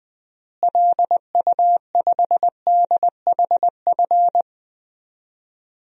Morse code, heard as a keyed tone, about 20 wpm, 700 Hz. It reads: LU5DHF
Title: DXCC-100 Call Signs - 6 Characters - Repeated 3x - 20wpm